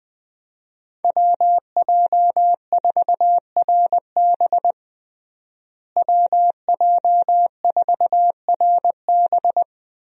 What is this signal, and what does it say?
Morse code, heard as a keyed tone, about 21 wpm, 700 Hz. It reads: WJ4RB WJ4RB
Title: USA Call Signs - 5 Characters - Repeated 3x - 20wpm